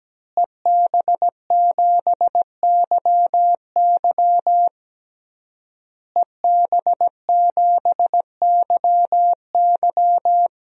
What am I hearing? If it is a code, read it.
Morse code, heard as a keyed tone, about 17 wpm, 700 Hz. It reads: EB7YY EB7YY
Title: DXCC-100 Call Signs - 5 Characters - Repeated 3x - 17wpm